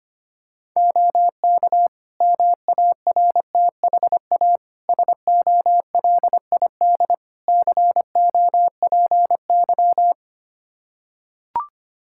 Morse code, heard as a keyed tone, about 25 wpm, 700 Hz. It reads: OK MARTHA SOLID COPY
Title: Practice QSO - Exchange 3 - More Info - 25wpm